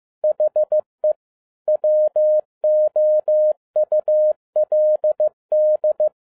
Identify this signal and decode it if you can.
Morse code, heard as a keyed tone, about 15 wpm, 610 Hz. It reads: HE WOULD
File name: 2011-07-mp3 / n47 JOHN K3TN 15wpm